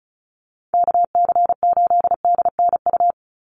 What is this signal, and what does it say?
Morse code, heard as a keyed tone, about 34 wpm, 700 Hz. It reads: KC8DNU
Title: USA Call Signs 35wpm